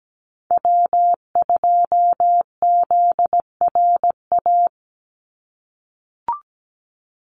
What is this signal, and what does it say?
Morse code, heard as a keyed tone, about 17 wpm, 700 Hz. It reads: W2ZRA
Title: Call Signs - 17wpm